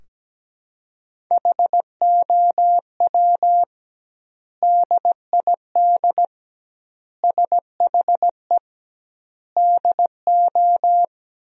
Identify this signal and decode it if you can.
Morse code, heard as a keyed tone, about 17 wpm, 700 Hz. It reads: HOW DID SHE DO
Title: 2x Word-Spacing - Sentences from Top 100 Words - 17wpm